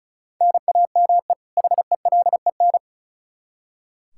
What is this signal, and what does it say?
Morse code, heard as a keyed tone, about 36 wpm, 700 Hz. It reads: NAME HELEN
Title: Practice QSO - Exchange 1 - Beginning - 35wpm